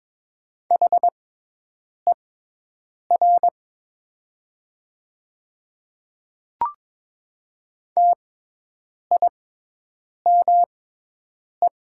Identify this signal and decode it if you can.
Morse code, heard as a keyed tone, about 22 wpm, spaced as for 8 wpm, 700 Hz. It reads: HER TIME
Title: 6x Character Spacing - Top 200 Words - 22wpm